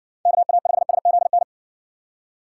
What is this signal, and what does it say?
Morse code, heard as a keyed tone, about 55 wpm, 700 Hz. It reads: KA5SZN